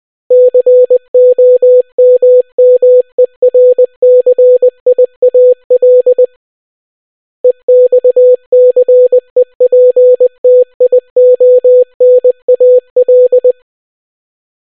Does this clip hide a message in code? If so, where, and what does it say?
Yes: Morse code from 0.3 to 13.5 s, heard as a keyed tone, about 20 wpm, 500 Hz. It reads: COMMERCIAL EXCEPTIONAL